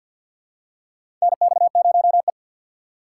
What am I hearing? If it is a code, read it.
Morse code, heard as a keyed tone, about 52 wpm, 700 Hz. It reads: NX0E
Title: USA Call Signs - 6x Character Spacing - 50wpm